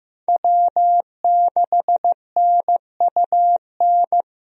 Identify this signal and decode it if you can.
Morse code, heard as a keyed tone, about 15 wpm, 700 Hz. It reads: W6NUN
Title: USA Call Signs - 5 Characters - Repeated 2x - 15wpm